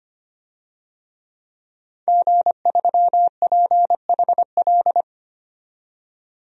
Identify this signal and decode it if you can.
Morse code, heard as a keyed tone, about 25 wpm, 700 Hz. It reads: G3PHL
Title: DXCC-100 Call Signs - 5 Characters - 25wpm